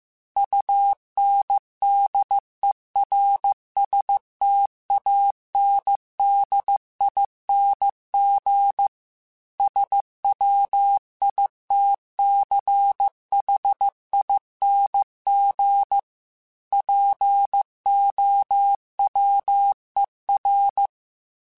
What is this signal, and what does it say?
Morse code, heard as a keyed tone, about 15 wpm, 800 Hz. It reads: UNDERSTANDING SWITCHING POWER
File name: Technical Subjects / Technical Subjects-mp3s / Sw Pwr Sup @13